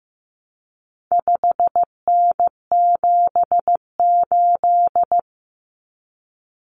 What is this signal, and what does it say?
Morse code, heard as a keyed tone, about 15 wpm, 700 Hz. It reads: 5N78
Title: Common Cut Numbers - 4-Digit 15wpm